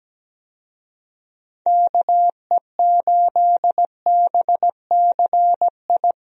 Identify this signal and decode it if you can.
Morse code, heard as a keyed tone, about 17 wpm, 700 Hz. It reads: KE8BCI
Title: USA Call Signs - 6 Characters - 17wpm